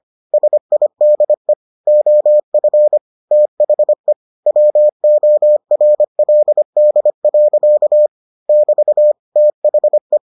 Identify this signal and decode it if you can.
Morse code, heard as a keyed tone, about 25 wpm, 600 Hz. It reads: SIDE OF THE WORLD. <BT> THE